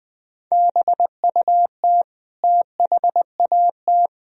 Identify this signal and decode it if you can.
Morse code, heard as a keyed tone, about 21 wpm, 700 Hz. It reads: BUT THAT
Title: Speed-Racing - Sets of 2 Words 17wpm